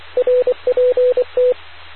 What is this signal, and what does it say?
Morse code, heard as a keyed tone, about 24 wpm, 500 Hz. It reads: RPT